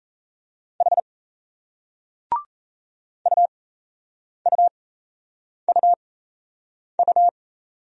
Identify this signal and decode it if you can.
Morse code, heard as a keyed tone, about 37 wpm, spaced as for 8 wpm, 700 Hz. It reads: H UUUU